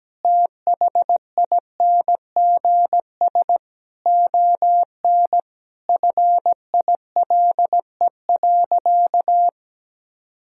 Morse code, heard as a keyed tone, about 17 wpm, 700 Hz. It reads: THINGS ON FILE.
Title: Sentences from Top 1000 Words - Review Part 4 - 17wpm